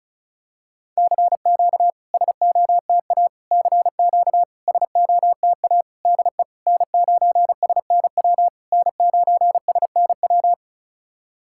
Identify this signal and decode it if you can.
Morse code, heard as a keyed tone, about 36 wpm, 700 Hz. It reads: CQ SOTA CQ SOTA DE N9SNW N9SNW
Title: SOTA QSO Practice 35wpm